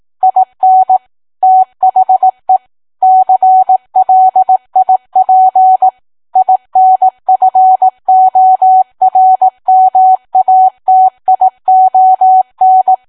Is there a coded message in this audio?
Yes: Morse code from 0.2 to 12.9 s, heard as a keyed tone, about 18 wpm, 750 Hz. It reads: IN THE CLIP INFORMATION